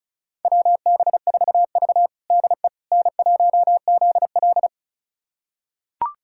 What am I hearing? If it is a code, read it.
Morse code, heard as a keyed tone, about 36 wpm, 700 Hz. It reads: WB4V DE N1ZL
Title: USA Callsign DE Callsign - 4 Characters - 35wpm